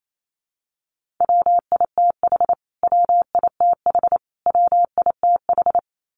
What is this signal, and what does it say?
Morse code, heard as a keyed tone, about 28 wpm, 700 Hz. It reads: WITH WITH WITH
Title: Top 200 Words Repeated 28wpm